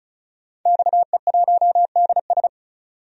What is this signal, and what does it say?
Morse code, heard as a keyed tone, about 36 wpm, 700 Hz. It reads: XE1DS
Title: DXCC-100 Call Signs - 5 Characters - 35wpm